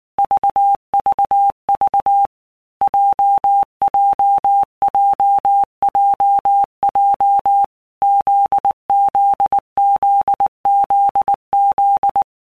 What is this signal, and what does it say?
Morse code, heard as a keyed tone, about 19 wpm, 800 Hz. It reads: VVV JJJJJ ZZZZZ
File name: All Groups 1-minute Blocks / Groups@18 / JZ17